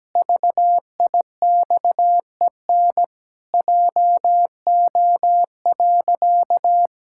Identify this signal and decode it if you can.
Morse code, heard as a keyed tone, about 17 wpm, 700 Hz. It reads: VIXEN JO.
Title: Pangram Challenge v2 - 17wpm